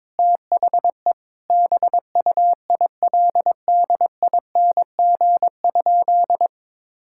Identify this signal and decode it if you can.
Morse code, heard as a keyed tone, about 22 wpm, 700 Hz. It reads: THE BUILDING?
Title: Sentences from Top 1000 Words - Review Part 5 - 22wpm